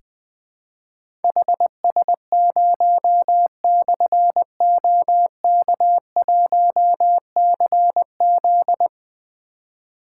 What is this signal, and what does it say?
Morse code, heard as a keyed tone, about 20 wpm, 700 Hz. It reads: HS0/OK1CZ
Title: Difficult Call Signs 20wpm